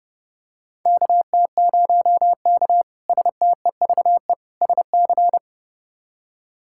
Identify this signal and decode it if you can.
Morse code, heard as a keyed tone, about 32 wpm, 700 Hz. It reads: KT0K STEVE SC
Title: K1USN SST QSO Practice - 30wpm